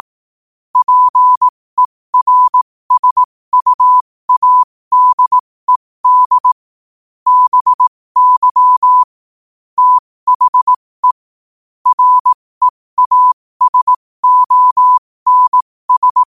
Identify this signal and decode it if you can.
Morse code, heard as a keyed tone, about 18 wpm, spaced as for 15 wpm, 1000 Hz. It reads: PERSUADED BY THE REASONS
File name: qotdfiles / 15wpm